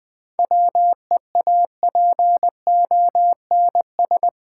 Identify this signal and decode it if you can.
Morse code, heard as a keyed tone, about 20 wpm, 700 Hz. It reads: WEAPONS
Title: Sets of 5 Words Encore - New Word Drill - 20wpm